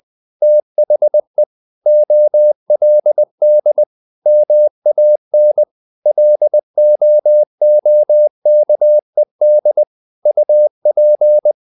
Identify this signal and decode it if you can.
Morse code, heard as a keyed tone, about 20 wpm, 600 Hz. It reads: THE OLD MAN LOOKED UP
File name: the old man and the sea / the old man and the sea part 1 20wpm